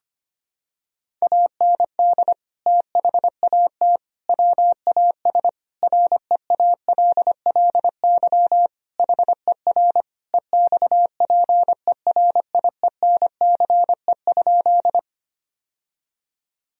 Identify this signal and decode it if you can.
Morse code, heard as a keyed tone, about 25 wpm, 700 Hz. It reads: AND THAT WAS REALLY HER EXPERIENCE?
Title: Sentences from Top 1000 Words - Review Part 3 - 25wpm